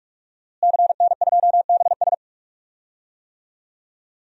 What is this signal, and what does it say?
Morse code, heard as a keyed tone, about 46 wpm, 700 Hz. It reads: CN2BS
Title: DXCC-100 Call Signs - 5 Characters - Repeated 2x - 45wpm